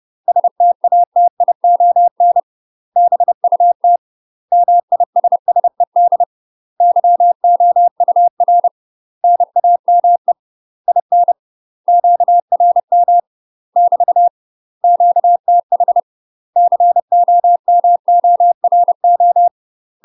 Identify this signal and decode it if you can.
Morse code, heard as a keyed tone, about 30 wpm, 700 Hz. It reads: STATION BUT MISSED YOUR NAME IN QRM <BT> QTH COMORO
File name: UK MASTER QSOs TRAINING / 30WPM / UK MASTER QSOs 0024